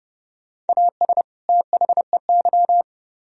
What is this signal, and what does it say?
Morse code, heard as a keyed tone, about 37 wpm, 700 Hz. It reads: AS THEY
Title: Speed-Racing - Sets of 2 Words 30wpm